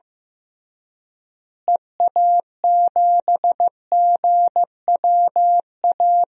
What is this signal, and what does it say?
Morse code, heard as a keyed tone, about 15 wpm, 700 Hz. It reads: EA7GWA
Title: DXCC-100 Call Signs - 6 Characters - Repeated 3x - 15wpm